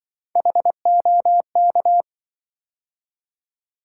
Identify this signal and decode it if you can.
Morse code, heard as a keyed tone, about 25 wpm, 700 Hz. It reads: HOK
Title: Speed-Racing - Three Letter-Number Combinations - 20wpm